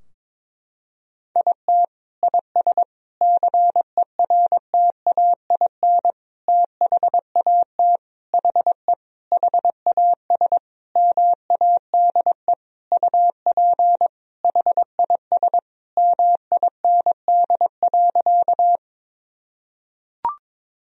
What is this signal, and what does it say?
Morse code, heard as a keyed tone, about 22 wpm, 700 Hz. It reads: IT IS CERTAIN THAT HE HAS MADE UP HIS MIND.
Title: Sentences from Top 1000 Words - Review Part 2 - 22wpm